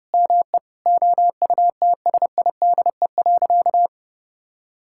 Morse code, heard as a keyed tone, about 30 wpm, 700 Hz. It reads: ME OUTSIDE.
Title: Sentences from Top 1000 Words - Review Part 6 - 30wpm